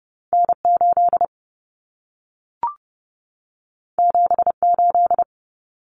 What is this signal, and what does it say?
Morse code, heard as a keyed tone, about 30 wpm, 700 Hz. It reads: N8 78
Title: Common Cut Numbers - 2-Digit 30wpm